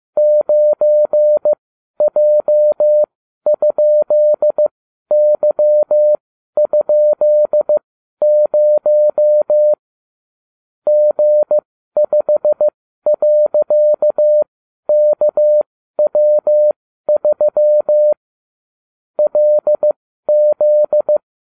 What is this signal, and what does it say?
Morse code, heard as a keyed tone, about 15 wpm, spaced as for 12 wpm, 600 Hz. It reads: 9J?Y?0 G5.KW3 LZ